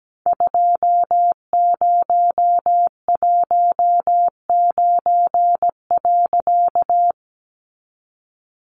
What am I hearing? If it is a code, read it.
Morse code, heard as a keyed tone, about 17 wpm, 700 Hz. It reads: 2019.
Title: A Sound Of Thunder 17wpm